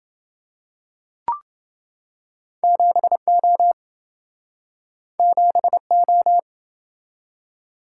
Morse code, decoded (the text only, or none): 7O 7O